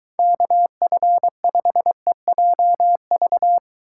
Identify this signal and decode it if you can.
Morse code, heard as a keyed tone, about 23 wpm, 700 Hz. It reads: KF5EJV